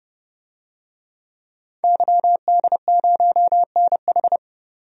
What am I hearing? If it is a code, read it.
Morse code, heard as a keyed tone, about 30 wpm, 700 Hz. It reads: YD0NH
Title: DXCC-100 Call Signs - 5 Characters - 30wpm